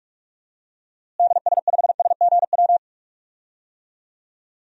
Morse code, heard as a keyed tone, about 46 wpm, 700 Hz. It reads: DS5SGW
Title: DXCC-100 Call Signs - 6 Characters - Repeated 3x - 45wpm